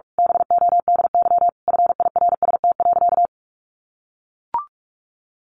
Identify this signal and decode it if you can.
Morse code, heard as a keyed tone, about 44 wpm, 700 Hz. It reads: BODY FIRST.
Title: Sentences from Top 1400 Words 45wpm